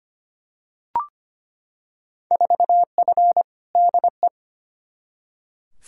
Morse code, heard as a keyed tone, about 25 wpm, 700 Hz. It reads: 4F DE
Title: Field Day Exchanges Encore - 25wpm